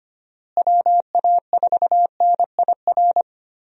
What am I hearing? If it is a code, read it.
Morse code, heard as a keyed tone, about 25 wpm, 700 Hz. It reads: WA4NIR
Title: USA Call Signs - 6 Characters - Repeated 3x - 25wpm